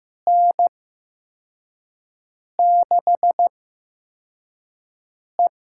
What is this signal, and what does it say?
Morse code, heard as a keyed tone, about 15 wpm, spaced as for 4 wpm, 700 Hz. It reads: N6E
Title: USA Call Signs - 8x Character Spacing - 15wpm